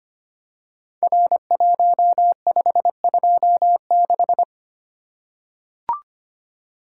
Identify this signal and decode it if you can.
Morse code, heard as a keyed tone, about 25 wpm, 700 Hz. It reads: R1526
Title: Code Groups - 5 Letter:Numbers - 25wpm